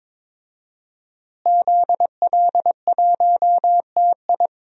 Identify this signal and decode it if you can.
Morse code, heard as a keyed tone, about 22 wpm, 700 Hz. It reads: ZL1TI
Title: DXCC-100 Call Signs - 5 Characters - Repeated 2x - 22wpm